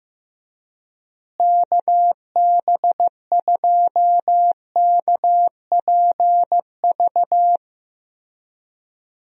Morse code, heard as a keyed tone, about 15 wpm, 700 Hz. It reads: KB2KPV
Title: USA Call Signs - 6 Characters - Repeated 3x - 15wpm